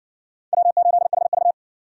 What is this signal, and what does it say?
Morse code, heard as a keyed tone, about 60 wpm, 700 Hz. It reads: W8HV